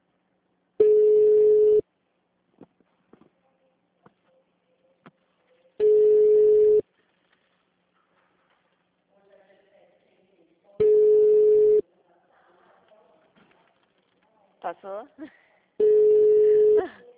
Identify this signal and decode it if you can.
Ringback tone (Europe, 425 Hz), with speech.